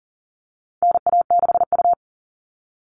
Morse code, heard as a keyed tone, about 40 wpm, 700 Hz. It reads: NA6U
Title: All Cut Numbers - 4-Digit 40wpm